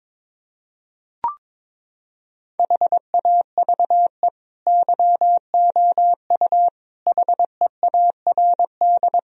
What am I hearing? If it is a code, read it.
Morse code, heard as a keyed tone, about 22 wpm, 700 Hz. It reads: HAVE YOU HEARD
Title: Sentences from Top 1000 Words - Review Part 3 - 22wpm